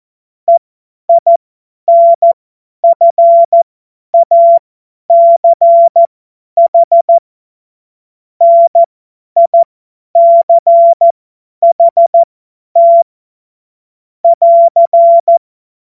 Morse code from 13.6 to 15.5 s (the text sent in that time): <AR>